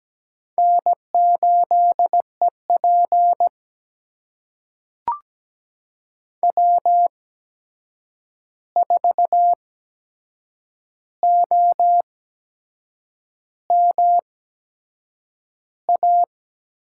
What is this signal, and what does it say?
Morse code, heard as a keyed tone, about 17 wpm, spaced as for 7 wpm, 700 Hz. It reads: N8EP W4OM A
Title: USA Call Signs - 8x Character Spacing - 17wpm